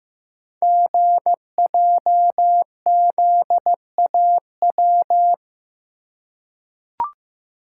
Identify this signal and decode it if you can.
Morse code, heard as a keyed tone, about 15 wpm, 700 Hz. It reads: GJZAW
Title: Code Groups - 5 Letters - 15wpm